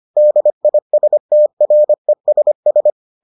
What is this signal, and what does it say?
Morse code, heard as a keyed tone, about 25 wpm, 600 Hz. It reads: DISTRESS